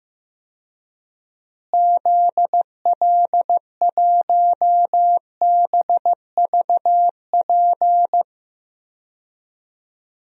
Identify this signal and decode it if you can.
Morse code, heard as a keyed tone, about 16 wpm, 700 Hz. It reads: ZL1BVP
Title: DXCC-100 Call Signs - 6 Characters - Repeated 2x - 15wpm